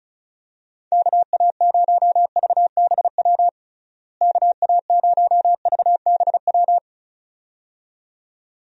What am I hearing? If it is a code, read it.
Morse code, heard as a keyed tone, about 34 wpm, 700 Hz. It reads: KA0VBW KA0VBW
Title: USA Call Signs - 6 Characters - Repeated 2x - 35wpm